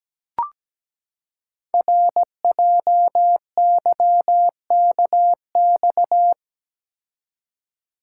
Morse code, heard as a keyed tone, about 17 wpm, 700 Hz. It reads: RJYKX
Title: Code Groups - 5 Letters - 17wpm